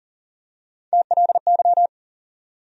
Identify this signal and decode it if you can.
Morse code, heard as a keyed tone, about 40 wpm, 700 Hz. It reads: TLY
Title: USA Call Sign Postfixes - 3 Letters 40wpm